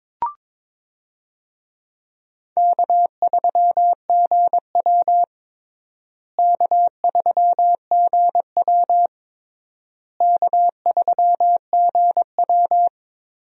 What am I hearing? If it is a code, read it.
Morse code, heard as a keyed tone, about 22 wpm, 700 Hz. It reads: K3GW K3GW K3GW
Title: USA Call Signs - 4 Characters - Repeated 3x - 22wpm